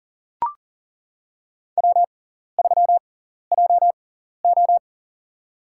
Morse code, heard as a keyed tone, about 40 wpm, spaced as for 14 wpm, 700 Hz. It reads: W3JO
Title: USA Call Signs - 4 Characters - 6x Character Spacing - 40wpm